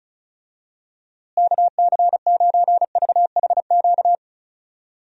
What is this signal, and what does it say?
Morse code, heard as a keyed tone, about 36 wpm, 700 Hz. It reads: KC9VHQ
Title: USA Call Signs - 6 Characters - Repeated 2x - 35wpm